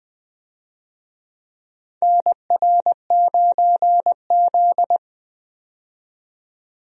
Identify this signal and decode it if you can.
Morse code, heard as a keyed tone, about 21 wpm, 700 Hz. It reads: NR9Z
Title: USA Call Signs - 4 Characters - 20wpm